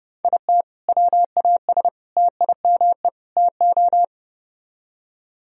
Morse code, heard as a keyed tone, about 32 wpm, 700 Hz. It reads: IT WAS TIME TO